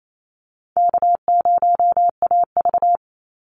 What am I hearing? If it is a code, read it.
Morse code, heard as a keyed tone, about 28 wpm, 700 Hz. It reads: K0AV
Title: Easy Call Signs - 28wpm